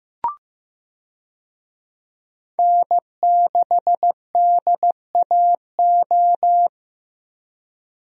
Morse code, heard as a keyed tone, about 15 wpm, 700 Hz. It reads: N6DAO
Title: USA Call Signs - 5 Characters - Repeated 2x - 15wpm